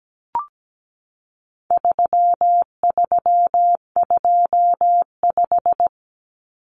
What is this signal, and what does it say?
Morse code, heard as a keyed tone, about 17 wpm, 700 Hz. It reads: 3325
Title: Common Cut Numbers - 4-Digit 17wpm